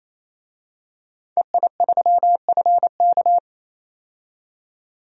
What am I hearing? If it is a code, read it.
Morse code, heard as a keyed tone, about 28 wpm, 700 Hz. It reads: EI3FK
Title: DXCC-100 Call Signs - 5 Characters - 28wpm